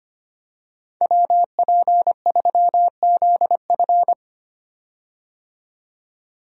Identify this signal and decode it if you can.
Morse code, heard as a keyed tone, about 25 wpm, 700 Hz. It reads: WP3ZF
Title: USA Call Signs - 5 Characters - 25wpm